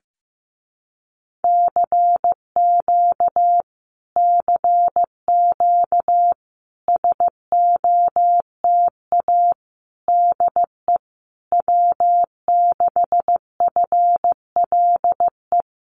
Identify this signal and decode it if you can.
Morse code, heard as a keyed tone, about 15 wpm, 700 Hz. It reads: CQ CQ SOTA DE W6FLE
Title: Common QSO Phrases Encore 15wpm